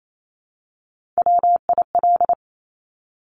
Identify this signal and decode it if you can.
Morse code, heard as a keyed tone, about 28 wpm, 690 Hz. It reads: WIL
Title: Top Three Letter Combinations in Words 28wpm